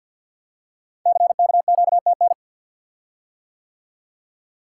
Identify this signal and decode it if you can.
Morse code, heard as a keyed tone, about 53 wpm, 700 Hz. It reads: CKQTN
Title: Code Groups - 5 Letters - 50wpm